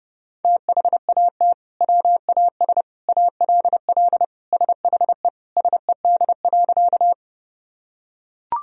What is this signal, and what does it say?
Morse code, heard as a keyed tone, about 30 wpm, 700 Hz. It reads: THAT WAS ALL SHE SED.
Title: Sentences from Top 100 Words and Abbreviations 30wpm